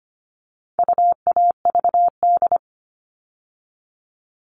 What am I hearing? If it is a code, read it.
Morse code, heard as a keyed tone, about 25 wpm, 700 Hz. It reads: UAVD
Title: All Cut Numbers - 4-Digit 25wpm